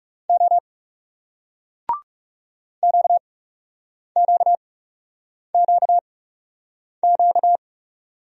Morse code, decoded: O QQQQ

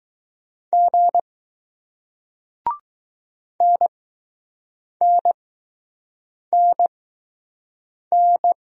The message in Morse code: G NNNN

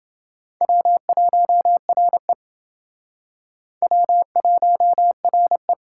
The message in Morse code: W1RE W1RE

